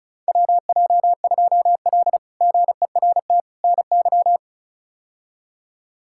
Morse code, heard as a keyed tone, about 34 wpm, 700 Hz. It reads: WJ2L GERT NY
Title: K1USN SST QSO Practice - 35wpm